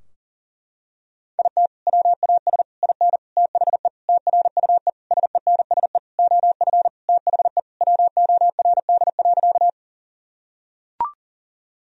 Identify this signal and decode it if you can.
Morse code, heard as a keyed tone, about 40 wpm, 700 Hz. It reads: IT WAS IN THE TRUE SENSE OF THE WORD.
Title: Sentences from Top 1000 Words - Review Part 4 - 40wpm